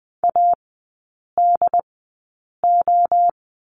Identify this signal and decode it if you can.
Morse code, heard as a keyed tone, about 20 wpm, spaced as for 8 wpm, 700 Hz. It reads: ADO
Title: 067 Lesson 12e - Letter d - Three Characters 20wpm